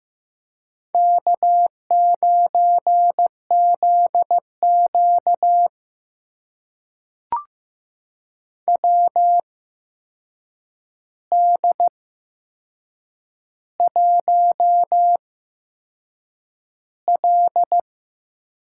Morse code, heard as a keyed tone, about 15 wpm, spaced as for 6 wpm, 700 Hz. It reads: K9ZQ WD1L
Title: USA Call Signs - 4 Characters - 8x Character Spacing - 15wpm